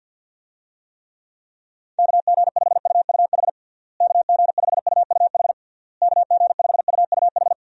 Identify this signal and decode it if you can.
Morse code, heard as a keyed tone, about 46 wpm, 700 Hz. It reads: KG5UUH KG5UUH KG5UUH
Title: USA Call Signs - 6 Characters - Repeated 3x - 50wpm